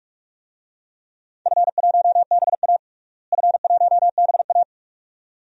Morse code, heard as a keyed tone, about 46 wpm, 700 Hz. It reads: F1BA F1BA
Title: DXCC-100 Call Signs - 4 Characters - Repeated 2x - 45wpm